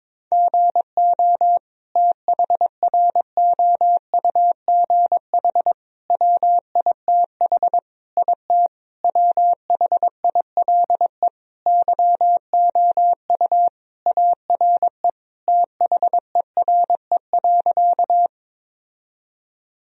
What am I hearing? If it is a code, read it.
Morse code, heard as a keyed tone, about 21 wpm, 700 Hz. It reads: GO THROUGH WITH IT WHILE YOU ARE THERE.
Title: Sentences from Top 1000 Words - Review Part 1 - 22wpm